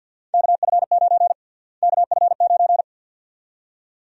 Morse code, heard as a keyed tone, about 50 wpm, 700 Hz. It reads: KF9 KF9
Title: USA Call Sign Prefixes - Repeated 2x - 50wpm